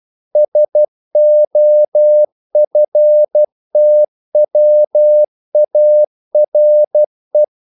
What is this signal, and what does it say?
Morse code, heard as a keyed tone, about 12 wpm, 600 Hz. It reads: SOFTWARE
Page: English quotations in CW (telegraphy - morse)